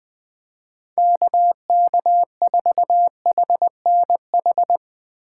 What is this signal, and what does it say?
Morse code, heard as a keyed tone, about 22 wpm, 700 Hz. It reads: KK4HNH
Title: Speed Racing - USA 2x3 Call Signs 20wpm